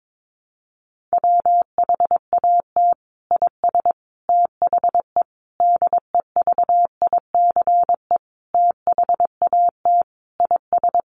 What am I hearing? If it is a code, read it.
Morse code, heard as a keyed tone, about 22 wpm, 700 Hz. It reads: WHAT IS THE DEVICE THAT IS
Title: Sentences from Top 1500 Words 22wpm